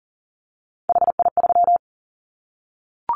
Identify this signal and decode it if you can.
Morse code, heard as a keyed tone, about 39 wpm, 700 Hz. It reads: HI3
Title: Callsign Prefixes for Top 100 DXCC - 40wpm